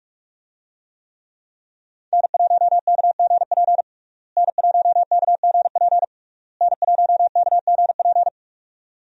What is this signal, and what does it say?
Morse code, heard as a keyed tone, about 41 wpm, 700 Hz. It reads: N1KGP N1KGP N1KGP
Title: USA Call Signs - 5 Characters - Repeated 3x - 45wpm